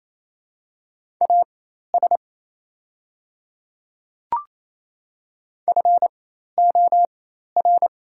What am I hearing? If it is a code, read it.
Morse code, heard as a keyed tone, about 28 wpm, spaced as for 13 wpm, 700 Hz. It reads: AS FOR
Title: 4x Character Spacing - Top 100 Words - 28wpm